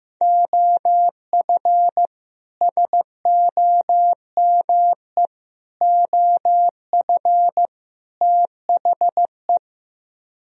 Morse code, heard as a keyed tone, about 15 wpm, 700 Hz. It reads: OF SOME OF THE
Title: Speed-Racing - Sets of 4 Words 15wpm